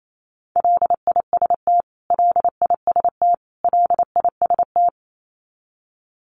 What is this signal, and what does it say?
Morse code, heard as a keyed tone, about 28 wpm, 700 Hz. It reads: LIST LIST LIST
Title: Top 500 Words Repeated 28wpm